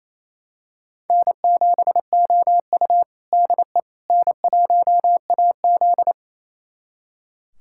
Morse code, heard as a keyed tone, about 28 wpm, 700 Hz. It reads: N7OU DE N1AZ
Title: USA Callsign DE Callsign - 4 Characters - 28wpm